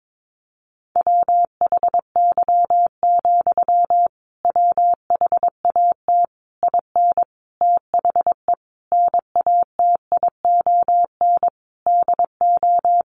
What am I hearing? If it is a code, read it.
Morse code, heard as a keyed tone, about 22 wpm, 700 Hz. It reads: WHY, WHAT IN THE NATION DO
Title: Sentences from Top 900 Words 22wpm